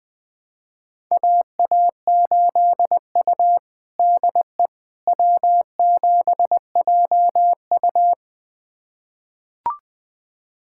Morse code, heard as a keyed tone, about 20 wpm, 700 Hz. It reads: AA8U DE W7JU
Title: USA Callsign DE Callsign 20wpm